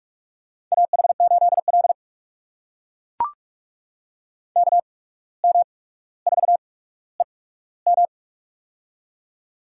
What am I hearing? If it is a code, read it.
Morse code, heard as a keyed tone, about 44 wpm, spaced as for 18 wpm, 700 Hz. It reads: AH8L KM4EM
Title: USA Call Signs - 8x Character Spacing - 45wpm